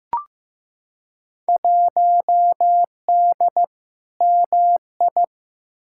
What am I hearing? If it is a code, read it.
Morse code, heard as a keyed tone, about 15 wpm, 700 Hz. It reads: 1D MI
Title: Field Day Exchanges - 15wpm